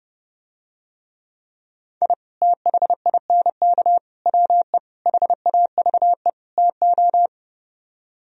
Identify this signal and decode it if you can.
Morse code, heard as a keyed tone, about 29 wpm, 700 Hz. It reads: I THINK WE HAVE TO